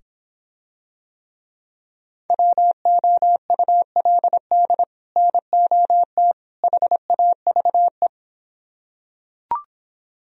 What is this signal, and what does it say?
Morse code, heard as a keyed tone, about 26 wpm, 700 Hz. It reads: WOULD NOT HAVE